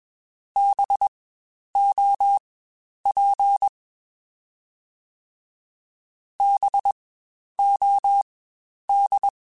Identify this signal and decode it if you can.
Morse code, heard as a keyed tone, about 21 wpm, spaced as for 10 wpm, 800 Hz. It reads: BOP BOD
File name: Spaced Alphas / Spaced Alphas mp3s / Bs sp @20